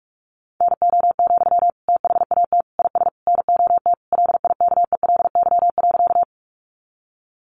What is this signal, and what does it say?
Morse code, heard as a keyed tone, about 44 wpm, 700 Hz. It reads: NO, THAT IS NOT LIKELY.
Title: Sentences from Top 600 Words 45wpm